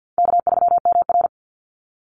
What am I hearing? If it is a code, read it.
Morse code, heard as a keyed tone, about 50 wpm, 700 Hz. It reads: K3MR
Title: Call Signs Repeated 3x - 50wpm